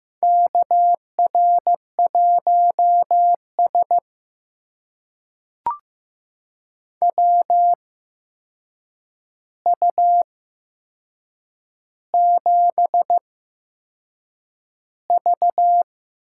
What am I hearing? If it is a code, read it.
Morse code, heard as a keyed tone, about 15 wpm, spaced as for 6 wpm, 700 Hz. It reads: KR1S WU7V
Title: USA Call Signs - 4 Characters - 8x Character Spacing - 15wpm